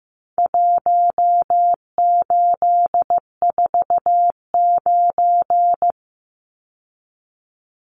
Morse code, heard as a keyed tone, about 15 wpm, 700 Hz. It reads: 1849
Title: Numbers - Contest Style - 4-Digit 15wpm